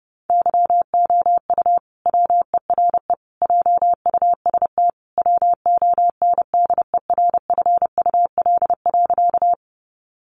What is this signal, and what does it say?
Morse code, heard as a keyed tone, about 30 wpm, 700 Hz. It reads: YOU WERE JUST WONDERFUL.
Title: Sentences from Top 1100 Words 30wpm